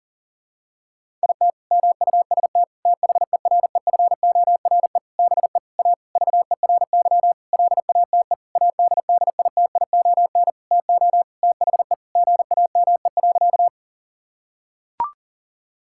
Morse code, heard as a keyed tone, about 40 wpm, 700 Hz. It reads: IT MUST THEREFORE BE A VERY LATE ADDITION TO THE GAME.
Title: Sentences from Top 1000 Words - Review Part 6 - 40wpm